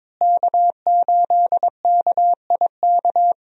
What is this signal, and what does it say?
Morse code, heard as a keyed tone, about 21 wpm, 700 Hz. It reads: K8KIK
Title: USA Call Signs - 5 Characters - Repeated 2x - 22wpm